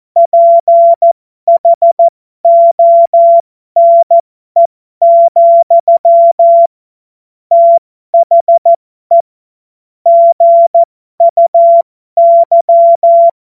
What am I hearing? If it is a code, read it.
Morse code, heard as a keyed tone, about 14 wpm, spaced as for 12 wpm, 680 Hz. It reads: PHONE, THE GUY